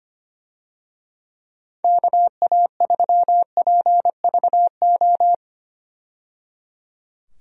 Morse code, heard as a keyed tone, about 25 wpm, 700 Hz. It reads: KA3PVO
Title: USA Call Signs - 6 Characters - 25wpm